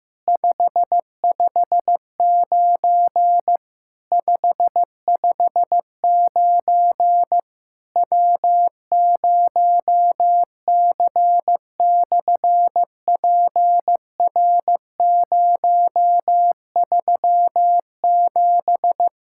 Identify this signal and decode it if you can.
Morse code, heard as a keyed tone, about 15 wpm, 700 Hz. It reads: 559 559 W0C/PR037
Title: SOTA QSO Practice 15wpm